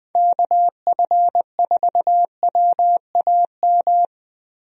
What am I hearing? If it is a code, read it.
Morse code, heard as a keyed tone, about 20 wpm, 700 Hz. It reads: KF4WAM
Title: USA Call Signs - 6 Characters - Repeated 2x - 20wpm